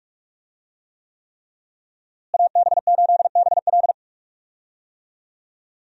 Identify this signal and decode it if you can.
Morse code, heard as a keyed tone, about 45 wpm, 700 Hz. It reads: AB8BL
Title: USA Call Signs - 5 Characters - Repeated 3x - 45wpm